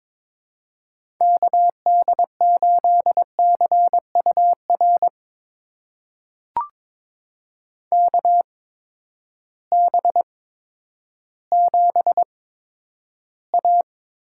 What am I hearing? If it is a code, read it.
Morse code, heard as a keyed tone, about 22 wpm, spaced as for 11 wpm, 700 Hz. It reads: KD8CUR KB7A
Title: USA Call Signs - 6 Characters - 8x Character Spacing - 22wpm